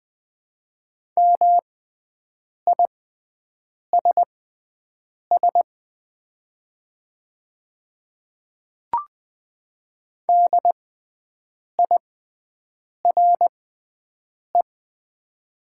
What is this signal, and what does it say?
Morse code, heard as a keyed tone, about 20 wpm, spaced as for 7 wpm, 700 Hz. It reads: MISS DIRE
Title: 6x Character Spacing - Top 500 Words - 20wpm